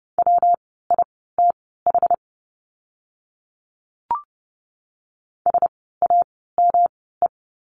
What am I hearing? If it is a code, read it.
Morse code, heard as a keyed tone, about 30 wpm, spaced as for 17 wpm, 700 Hz. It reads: WITH SAME
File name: course / 30:10wpm / 237 Lesson 38f - Letter q - Words 30wpm